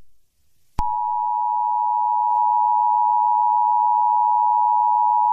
Emergency Alert System (SAME) tones (the 853 and 960 Hz attention tone). It is heard from 0.7 to 5.3 s.
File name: eas-alert-usa